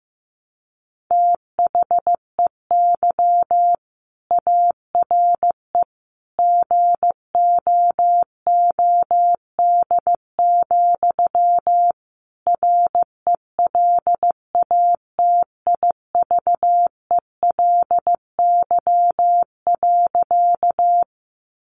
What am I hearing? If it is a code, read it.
Morse code, heard as a keyed tone, about 15 wpm, 700 Hz. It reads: THEY ARE GOOD, RELATIVELY.
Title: Sentences from Top 1100 Words 15wpm